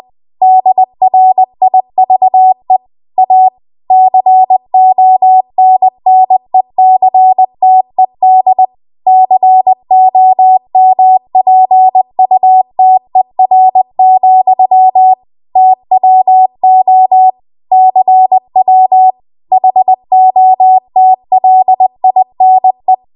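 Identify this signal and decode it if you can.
Morse code, heard as a keyed tone, about 20 wpm, 750 Hz. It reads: DRIVE A CONNECTED COMPUTER, TWO CW HOTLINE
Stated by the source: Morse code practice at 20 WPM.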